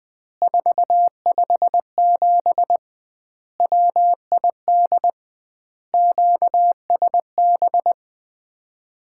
Morse code, heard as a keyed tone, about 21 wpm, 700 Hz. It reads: 457 WID QSB
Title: Common QSO Phrases - 2x Word-Spacing - 20wpm